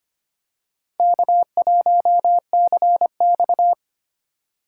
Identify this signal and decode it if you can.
Morse code, heard as a keyed tone, about 25 wpm, 700 Hz. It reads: K1CX
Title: K1USN SST QSO Practice - 25wpm